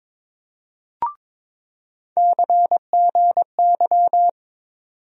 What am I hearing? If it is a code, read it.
Morse code, heard as a keyed tone, about 22 wpm, 700 Hz. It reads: CGY
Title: USA Call Sign Postfixes - 3 Letters 22wpm